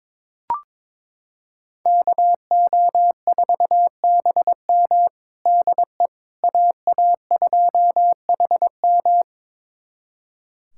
Morse code, heard as a keyed tone, about 22 wpm, 700 Hz. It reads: KO4BM DE AA2HM
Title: USA Callsign DE Callsign - 5 Characters - 22wpm